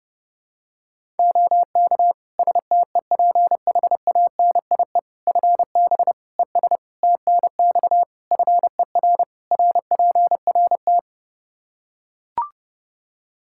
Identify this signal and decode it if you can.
Morse code, heard as a keyed tone, about 30 wpm, 700 Hz. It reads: OK STEPHANIE FB ES TNX FER RPRT
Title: Practice QSO - Exchange 2 - Info - 30wpm